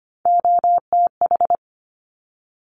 Morse code, heard as a keyed tone, about 25 wpm, 700 Hz. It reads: OTH
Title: Top Three Letter Combinations in Words 25wpm